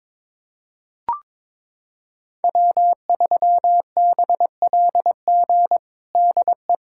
Morse code, heard as a keyed tone, about 21 wpm, 700 Hz. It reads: W3BLG DE
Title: USA Callsign DE Callsign - 5 Characters - 22wpm